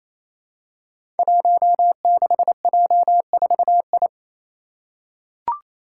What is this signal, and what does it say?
Morse code, heard as a keyed tone, about 28 wpm, 700 Hz. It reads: 16J4I
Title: Five Challenging Letter-Number Combinations 28wpm